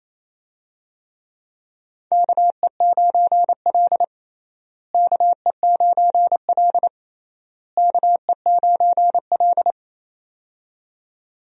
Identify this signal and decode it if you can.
Morse code, heard as a keyed tone, about 27 wpm, 700 Hz. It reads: KE9L KE9L KE9L
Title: USA Call Signs - 4 Characters - Repeated 3x - 28wpm